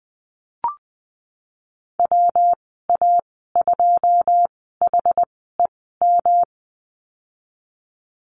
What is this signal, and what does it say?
Morse code, heard as a keyed tone, about 19 wpm, spaced as for 14 wpm, 700 Hz. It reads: WA2HEM
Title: 147 Lesson 24g - Letter y - Call Signs 20wpm